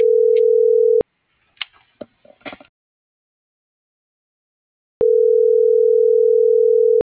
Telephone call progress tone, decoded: ringback tone

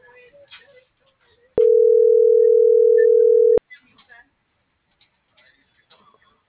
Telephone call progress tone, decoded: ringback tone

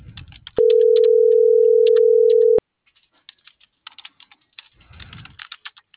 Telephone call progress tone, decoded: ringback tone

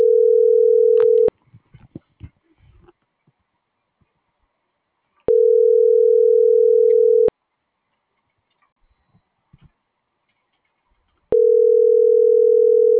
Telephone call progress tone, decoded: ringback tone

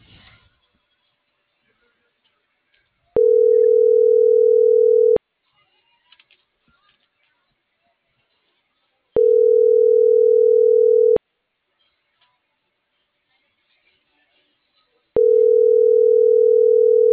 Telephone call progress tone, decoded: ringback tone

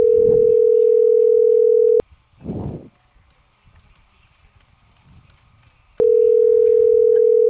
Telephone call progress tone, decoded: ringback tone

